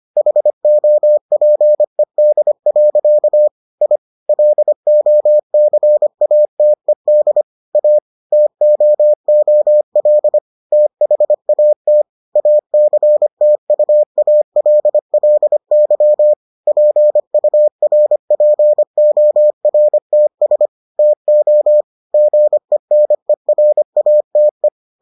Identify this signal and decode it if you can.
Morse code, heard as a keyed tone, about 25 wpm, 600 Hz. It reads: HOPED. I LOCATED A TOOL THAT ACTUALLY PURPORTS TO GENERATE